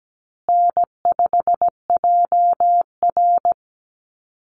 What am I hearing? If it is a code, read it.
Morse code, heard as a keyed tone, about 17 wpm, 700 Hz. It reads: N5JR
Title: Easy Call Signs Repeated 3x - 17wpm